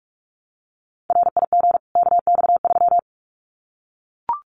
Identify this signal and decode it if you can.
Morse code, heard as a keyed tone, about 44 wpm, 700 Hz. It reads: RIG KX3
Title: Common QSO Phrases 45wpm